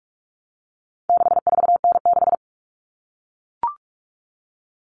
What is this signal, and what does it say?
Morse code, heard as a keyed tone, about 44 wpm, 700 Hz. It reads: 64N6
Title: All Cut Numbers - 4-Digit 45wpm